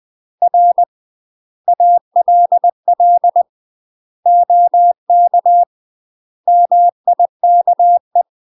Morse code, heard as a keyed tone, about 20 wpm, 700 Hz. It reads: R ALL OK MIKE
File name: UK MASTER QSOs TRAINING / 20WPM / UK MASTER QSOs 0048